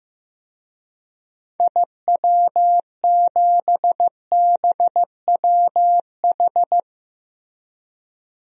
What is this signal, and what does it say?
Morse code, heard as a keyed tone, about 15 wpm, 700 Hz. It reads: IW7BWH
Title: DXCC-100 Call Signs - 6 Characters - Repeated 2x - 15wpm